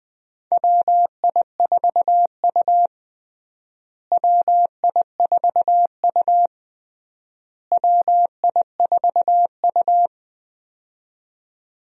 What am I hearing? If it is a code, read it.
Morse code, heard as a keyed tone, about 20 wpm, 700 Hz. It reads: WI4U WI4U WI4U
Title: USA Call Signs - 4 Characters - Repeated 3x - 20wpm